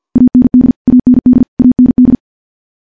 Morse code, heard as a keyed tone, about 50 wpm, 250 Hz. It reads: AND AND AND
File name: morse / morse-training-50WPM